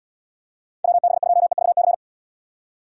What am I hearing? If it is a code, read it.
Morse code, heard as a keyed tone, about 78 wpm, 700 Hz. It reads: WB2EVC